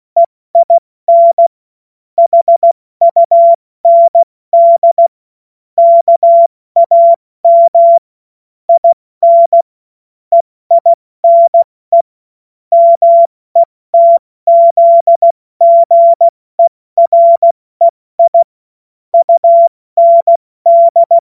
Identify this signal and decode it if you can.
Morse code, heard as a keyed tone, about 16 wpm, 680 Hz. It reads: EIN HUND KAM IN EINE METZGEREI UND
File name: Morsekurs / Witze / Witz 04. Oktober 2022 70 F80